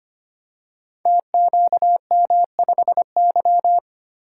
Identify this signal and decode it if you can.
Morse code, heard as a keyed tone, about 25 wpm, 700 Hz. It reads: TQM5Y
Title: Code Groups - 5 Letter:Numbers - 25wpm